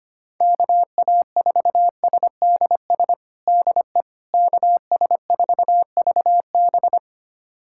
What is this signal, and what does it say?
Morse code, heard as a keyed tone, about 25 wpm, 700 Hz. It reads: KA4SDS DE KS4VB
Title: USA Callsign DE Callsign 25wpm